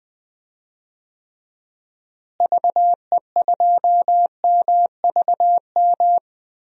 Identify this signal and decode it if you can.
Morse code, heard as a keyed tone, about 20 wpm, 700 Hz. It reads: VE2MVM
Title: DXCC-100 Call Signs - 6 Characters - Repeated 2x - 20wpm